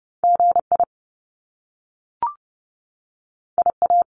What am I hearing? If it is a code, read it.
Morse code, heard as a keyed tone, about 30 wpm, 700 Hz. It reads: G I I A
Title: Top Two Letter Combinations in Words 30wpm